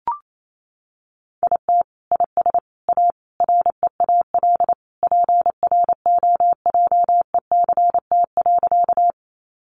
Morse code, heard as a keyed tone, about 28 wpm, 700 Hz. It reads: IT IS A REAL PROJECT.
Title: Sentences from Top 1400 Words 28wpm